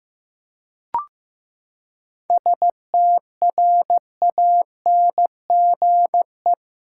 Morse code, heard as a keyed tone, about 15 wpm, 700 Hz. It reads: STRANGE
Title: Seven Letter Words - 15wpm